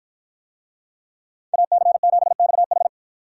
Morse code, heard as a keyed tone, about 52 wpm, 700 Hz. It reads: AC7XH